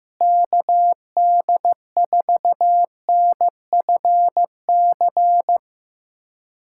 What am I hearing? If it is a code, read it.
Morse code, heard as a keyed tone, about 15 wpm, 700 Hz. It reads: KD4NFC
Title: USA Callsign DE Callsign - 6 Characters - 15wpm